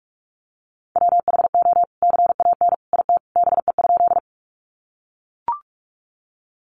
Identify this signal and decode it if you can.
Morse code, heard as a keyed tone, about 44 wpm, 700 Hz. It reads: WHO CAN IT BE?
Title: Sentences from Top 100 Words 45wpm